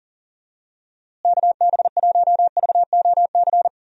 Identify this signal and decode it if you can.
Morse code, heard as a keyed tone, about 40 wpm, 700 Hz. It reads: KB1VOC
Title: USA Call Signs - 6 Characters - Repeated 2x - 40wpm